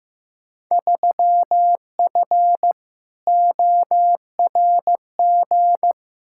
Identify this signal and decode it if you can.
Morse code, heard as a keyed tone, about 15 wpm, 700 Hz. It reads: 3F ORG
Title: Field Day Exchanges - 15wpm